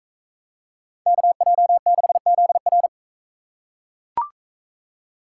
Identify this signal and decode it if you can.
Morse code, heard as a keyed tone, about 40 wpm, 700 Hz. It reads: KJ6ZR